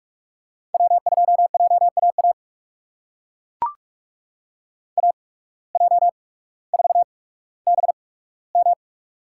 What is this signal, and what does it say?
Morse code, heard as a keyed tone, about 41 wpm, spaced as for 19 wpm, 700 Hz. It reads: W2JAA AJ4BM
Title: USA Call Signs - 5 Characters - 8x Character Spacing - 45wpm